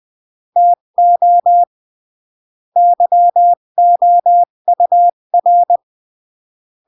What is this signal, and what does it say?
Morse code, heard as a keyed tone, about 20 wpm, 700 Hz. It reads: TO YOUR
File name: CW / jodorowsky / 82 commandments 20-15